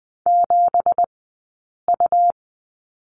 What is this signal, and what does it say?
Morse code, heard as a keyed tone, about 20 wpm, spaced as for 8 wpm, 700 Hz. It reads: 7U
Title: 165 Lesson 27d - Number 7 - Two Characters 20wpm